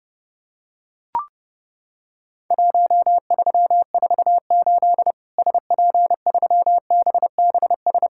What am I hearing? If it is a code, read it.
Morse code, heard as a keyed tone, about 31 wpm, 700 Hz. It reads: 1348 SP3BBS
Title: Worked All Europe DX Contest - QTC Exchange - 30wpm